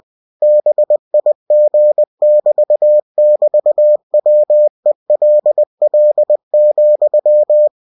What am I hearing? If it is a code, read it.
Morse code, heard as a keyed tone, about 20 wpm, 600 Hz. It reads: BIG<BT><BT>WELL,
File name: cw-books / winnie-the-pooh / winnie-20wpm-page-0043